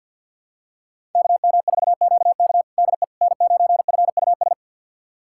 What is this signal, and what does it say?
Morse code, heard as a keyed tone, about 53 wpm, 700 Hz. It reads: KM4QK DE N9FUS
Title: USA Callsign DE Callsign - 5 Characters - 50wpm